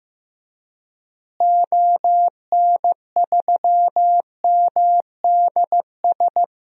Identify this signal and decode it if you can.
Morse code, heard as a keyed tone, about 15 wpm, 700 Hz. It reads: ON3MDS
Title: DXCC-100 Call Signs - 6 Characters - Repeated 2x - 15wpm